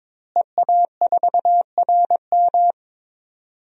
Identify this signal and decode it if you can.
Morse code, heard as a keyed tone, about 22 wpm, 700 Hz. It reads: EA4RM
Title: DXCC-100 Call Signs - 5 Characters - Repeated 3x - 22wpm